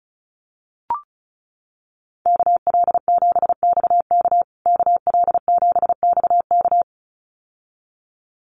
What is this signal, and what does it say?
Morse code, heard as a keyed tone, about 34 wpm, 700 Hz. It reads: KL7XK KL7XK
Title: USA Call Signs Repeated 2x 35wpm